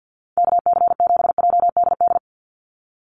Morse code, heard as a keyed tone, about 50 wpm, 700 Hz. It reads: KC7JDD
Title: USA Call Signs 50wpm